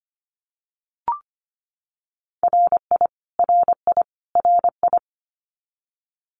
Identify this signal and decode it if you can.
Morse code, heard as a keyed tone, about 25 wpm, 700 Hz. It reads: RI RI RI
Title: US States Repeated 3x 25wpm